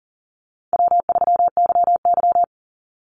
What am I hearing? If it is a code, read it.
Morse code, heard as a keyed tone, about 40 wpm, 700 Hz. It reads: W3YY
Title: Easy Call Signs - 40wpm